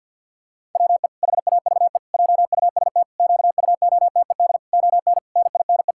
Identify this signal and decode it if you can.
Morse code, heard as a keyed tone, about 50 wpm, 700 Hz. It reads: WE HAVE JUST QUOTED ON NINE
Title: Pangram Challenge v2 - 50wpm